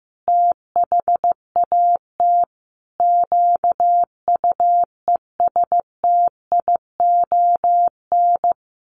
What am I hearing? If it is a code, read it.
Morse code, heard as a keyed tone, about 15 wpm, 700 Hz. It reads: THAT QUESTION
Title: Sets of 5 Words Encore2 15wpm